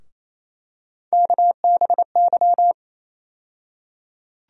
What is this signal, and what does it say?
Morse code, heard as a keyed tone, about 28 wpm, 700 Hz. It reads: KBY
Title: USA Call Sign Postfixes - 3 Letters 28wpm